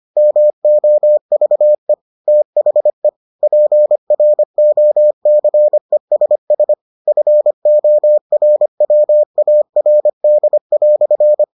Morse code, heard as a keyed tone, about 25 wpm, 600 Hz. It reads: MOVE THE PROCESS FORWARD"